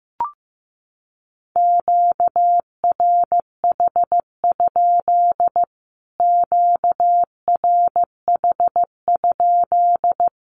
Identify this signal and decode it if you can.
Morse code, heard as a keyed tone, about 15 wpm, 700 Hz. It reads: QRH? QRH?
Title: Q-Codes Repeated 3x 15wpm